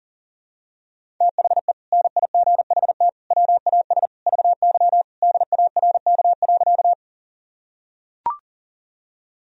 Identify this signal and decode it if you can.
Morse code, heard as a keyed tone, about 40 wpm, 700 Hz. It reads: THE NIGHT WAS VY DARK.
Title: Sentences from Top 500 Words and Abbreviations 40wpm